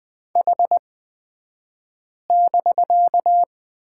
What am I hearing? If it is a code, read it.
Morse code, heard as a keyed tone, about 20 wpm, spaced as for 5 wpm, 700 Hz. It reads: H<BK>
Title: 249 Lesson 40d - BK - Two Characters 20wpm